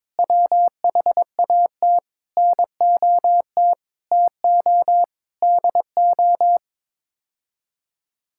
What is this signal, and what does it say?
Morse code, heard as a keyed tone, about 22 wpm, 700 Hz. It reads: WHAT NOT TO DO
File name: practice / 22wpm / Sets of 8 Words - 22wpm